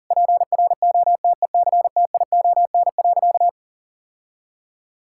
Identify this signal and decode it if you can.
Morse code, heard as a keyed tone, about 40 wpm, 700 Hz. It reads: PROTECTION.
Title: Sentences from Top 1600 Words 40wpm